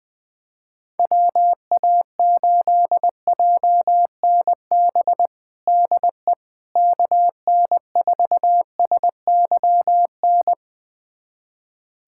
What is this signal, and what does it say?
Morse code, heard as a keyed tone, about 21 wpm, 700 Hz. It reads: WA8JNB DE KN4SYN
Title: USA Callsign DE Callsign - 6 Characters - 20wpm